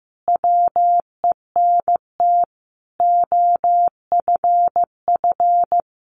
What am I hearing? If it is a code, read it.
Morse code, heard as a keyed tone, about 15 wpm, 700 Hz. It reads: WENT OFF
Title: Sentences from Top 800 Words 15wpm